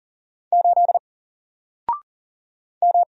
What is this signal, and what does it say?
Morse code, heard as a keyed tone, about 40 wpm, 700 Hz. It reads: 8 M
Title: Single Letter-Number Alt 40wpm